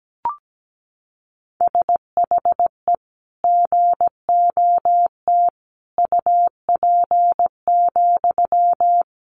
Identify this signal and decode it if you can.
Morse code, heard as a keyed tone, about 17 wpm, 700 Hz. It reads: SHE GOT UP,
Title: Sentences from Top 500 Words 17wpm